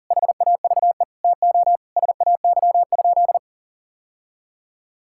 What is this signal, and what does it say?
Morse code, heard as a keyed tone, about 37 wpm, 700 Hz. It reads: HAVE TO SAY?
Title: Sentences from Top 1000 Words - Review Part 6 - 40wpm